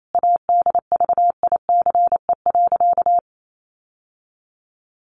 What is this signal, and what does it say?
Morse code, heard as a keyed tone, about 28 wpm, 700 Hz. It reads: ADVICE.
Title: Sentences from Top 1100 Words 28wpm